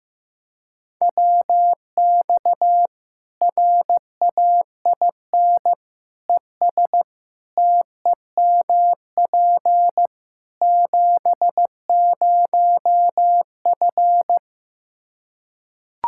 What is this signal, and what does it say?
Morse code, heard as a keyed tone, about 15 wpm, 700 Hz. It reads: WX RAIN ES TEMP 70F
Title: Practice QSO - Exchange 2 - Info - 15wpm